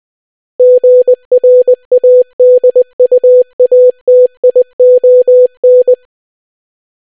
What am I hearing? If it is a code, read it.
Morse code, heard as a keyed tone, about 20 wpm, 500 Hz. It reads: GRADUATION